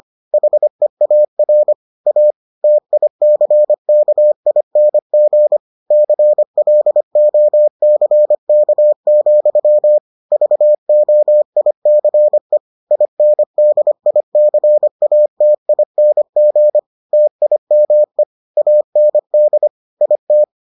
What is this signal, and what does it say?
Morse code, heard as a keyed tone, about 25 wpm, 600 Hz. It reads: HEAR A TICKING CLOCK, VOICE INDICATING TIME AND IT